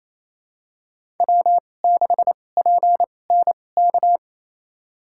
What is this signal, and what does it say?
Morse code, heard as a keyed tone, about 27 wpm, spaced as for 20 wpm, 700 Hz. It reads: W6PNK
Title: USA Call Signs - 5 Characters - 2x Character Spacing - 28wpm